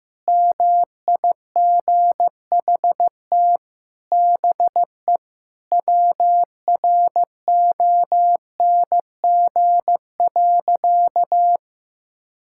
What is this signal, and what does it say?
Morse code, heard as a keyed tone, about 15 wpm, 700 Hz. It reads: MIGHT BE WRONG.
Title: Sentences from Top 1600 Words 15wpm